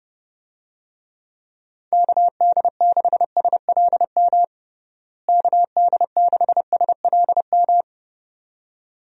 Morse code, heard as a keyed tone, about 30 wpm, 700 Hz. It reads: KD6SLM KD6SLM
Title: USA Call Signs - 6 Characters - Repeated 2x - 30wpm